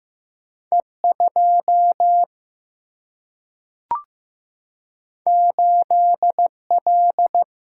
Morse code, heard as a keyed tone, about 15 wpm, 700 Hz. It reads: E2 8L
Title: Single-Letter Single-Number 15wpm